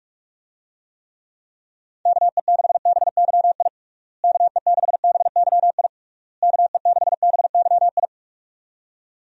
Morse code, heard as a keyed tone, about 41 wpm, 700 Hz. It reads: KE6BYI KE6BYI KE6BYI
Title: USA Call Signs - 6 Characters - Repeated 3x - 45wpm